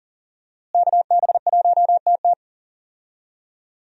Morse code, heard as a keyed tone, about 40 wpm, 700 Hz. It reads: KB1TT
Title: USA Call Signs - 5 Characters - Repeated 2x - 40wpm